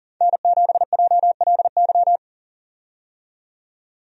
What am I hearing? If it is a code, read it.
Morse code, heard as a keyed tone, about 40 wpm, 700 Hz. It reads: N7JLY
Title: USA Call Signs - 5 Characters - 40wpm